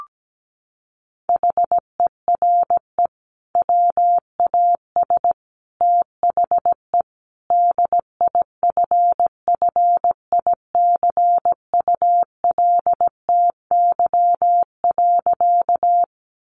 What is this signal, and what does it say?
Morse code, heard as a keyed tone, about 17 wpm, 700 Hz. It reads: HERE WAS THE DIFFICULTY.
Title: Sentences from Top 1500 Words 17wpm